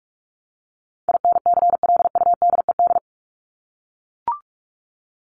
Morse code, heard as a keyed tone, about 44 wpm, 700 Hz. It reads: INCLUDED
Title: ED Suffix - Top 100 Words with Base Words 45wpm